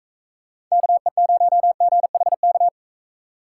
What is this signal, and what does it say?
Morse code, heard as a keyed tone, about 43 wpm, 700 Hz. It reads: KE0GHK